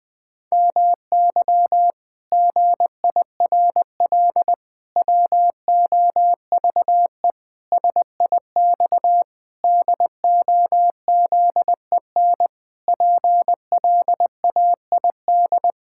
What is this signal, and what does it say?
Morse code, heard as a keyed tone, about 20 wpm, 700 Hz. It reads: MY GIRL WOVE SIX DOZEN PLAID
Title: Pangram Challenge v2 - 20wpm